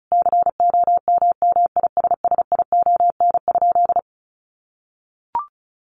Morse code, heard as a keyed tone, about 34 wpm, 700 Hz. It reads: COMMISSION?
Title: Sentences from Top 1400 Words 35wpm